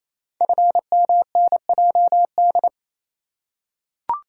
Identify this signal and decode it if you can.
Morse code, heard as a keyed tone, about 29 wpm, 700 Hz. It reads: FMNJD
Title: Code Groups - 5 Letters - 28wpm